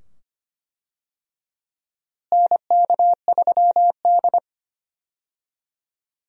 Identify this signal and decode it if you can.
Morse code, heard as a keyed tone, about 25 wpm, 700 Hz. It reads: NK3D
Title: USA Call Signs - 4 Characters - Repeated 2x - 25wpm